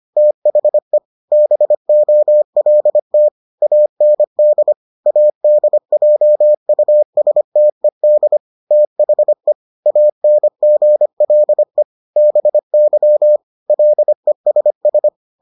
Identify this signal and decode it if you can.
Morse code, heard as a keyed tone, about 25 wpm, 600 Hz. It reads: THE BOLT AND ADJUSTED THE ANGLE BY LESS